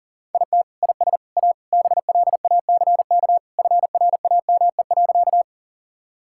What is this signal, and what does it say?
Morse code, heard as a keyed tone, about 37 wpm, 700 Hz. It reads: IT IS A BLACK FRAME.
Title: Sentences from Top 1000 Words - Review Part 6 - 40wpm